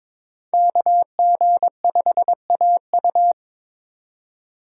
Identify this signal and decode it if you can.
Morse code, heard as a keyed tone, about 22 wpm, 700 Hz. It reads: KG5AU
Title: USA Call Signs - 5 Characters - 2x Character Spacing - 22wpm